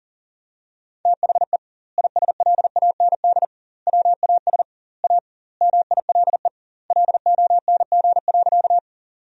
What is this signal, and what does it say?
Morse code, heard as a keyed tone, about 37 wpm, 700 Hz. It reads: THE ISLAND WAS A MILE LONG.
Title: 2x Word-Spacing - Sentences from Top 500 Words - 40wpm